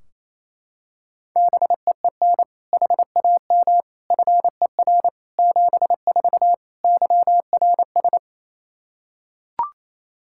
Morse code, heard as a keyed tone, about 28 wpm, 700 Hz. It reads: BEEN HAM FER 74 YRS
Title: Practice QSO - Exchange 3 - More Info - 28wpm